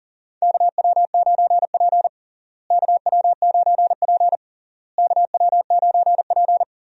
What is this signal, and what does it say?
Morse code, heard as a keyed tone, about 37 wpm, 700 Hz. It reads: KW9P KW9P KW9P
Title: USA Call Signs - 4 Characters - Repeated 3x - 40wpm